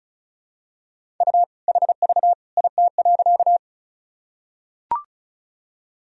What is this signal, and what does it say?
Morse code, heard as a keyed tone, about 36 wpm, 700 Hz. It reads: U HV IT.
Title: Sentences from Top 100 Words and Abbreviations 35wpm